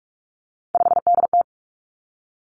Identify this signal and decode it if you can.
Morse code, heard as a keyed tone, about 46 wpm, 700 Hz. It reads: 5DT
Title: Three Challenging Letter-Number Combinations 45wpm